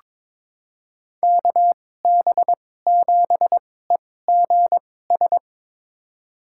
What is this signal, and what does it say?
Morse code, heard as a keyed tone, about 23 wpm, spaced as for 16 wpm, 700 Hz. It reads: KB7EGS
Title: USA Call Signs - 6 Characters - 2x Character Spacing - 22wpm